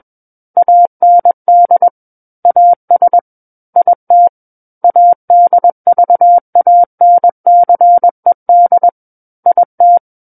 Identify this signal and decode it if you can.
Morse code, heard as a keyed tone, about 21 wpm, 700 Hz. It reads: AND AS IT ADVANCED IT
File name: Book One / 12-Chapter 12 - 21 WPM